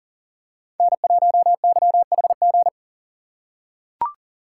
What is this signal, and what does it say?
Morse code, heard as a keyed tone, about 40 wpm, 700 Hz. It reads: N1YHG
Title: USA Call Signs - 5 Characters - 2x Character Spacing - 40wpm